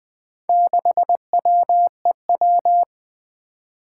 Morse code, heard as a keyed tone, about 20 wpm, 700 Hz. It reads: 6WEW
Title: Four Challenging Letter-Number Combinations 20wpm